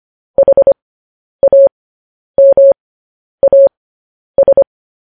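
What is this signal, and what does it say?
Morse code, heard as a keyed tone, about 25 wpm, spaced as for 10 wpm, 550 Hz. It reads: HAMAS
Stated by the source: Morse code news headlines sent at 10 words per minute.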